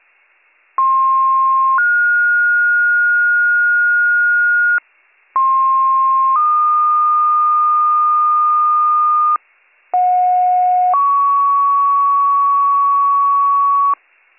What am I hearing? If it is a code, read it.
Two-tone page, heard as a one-second tone, then a three-second tone, about 1060 then 1430 Hz, about 1060 then 1220 Hz, about 730 then 1120 Hz. It is heard from 0.8 to 13.9 s.